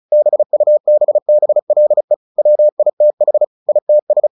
Morse code, heard as a keyed tone, about 34 wpm, 600 Hz. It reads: BUBBLE WITH ITS
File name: the old man and the sea / the old man and the sea 1 35wpm